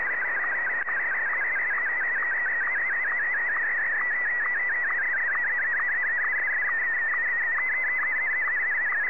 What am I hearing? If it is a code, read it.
SSTV (slow-scan television), Martin 1.